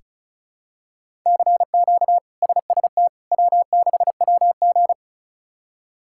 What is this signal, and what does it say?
Morse code, heard as a keyed tone, about 36 wpm, 700 Hz. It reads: CQ SST W6WG
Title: K1USN SST QSO Practice - 35wpm